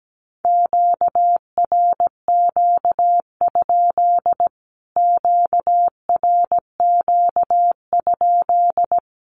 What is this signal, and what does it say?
Morse code, heard as a keyed tone, about 17 wpm, 700 Hz. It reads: QRQ? QRQ?
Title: Q-Codes Repeated 3x 17wpm